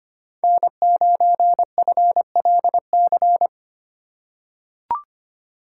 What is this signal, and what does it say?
Morse code, heard as a keyed tone, about 25 wpm, 700 Hz. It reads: N9FLC
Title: USA Call Signs - 5 Characters - 4x Character Spacing - 25wpm